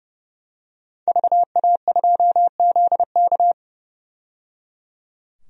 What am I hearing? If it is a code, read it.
Morse code, heard as a keyed tone, about 30 wpm, 700 Hz. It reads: VA2ZK
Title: DXCC-100 Call Signs - 5 Characters - 30wpm